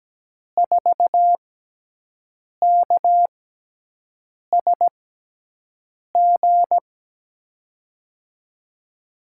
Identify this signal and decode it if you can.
Morse code, heard as a keyed tone, about 17 wpm, spaced as for 6 wpm, 700 Hz. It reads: 4KSG
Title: USA Call Signs - 6 Characters - 6x Character Spacing - 17wpm